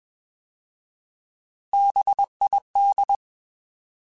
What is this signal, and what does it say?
Morse code, heard as a keyed tone, about 21 wpm, 800 Hz. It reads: BID
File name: Alphabetical / Alphabetical mp3s / Bs @20